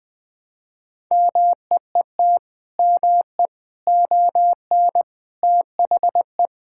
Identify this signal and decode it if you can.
Morse code, heard as a keyed tone, about 20 wpm, 700 Hz. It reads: MEET ME ON THE
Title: Sentences with 6-Word Phrases - 20wpm